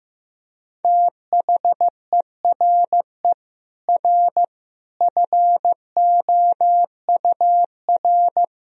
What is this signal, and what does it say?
Morse code, heard as a keyed tone, about 16 wpm, 700 Hz. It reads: THERE R FOUR
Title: Sentences from Top 400 Words and Abbreviations 15wpm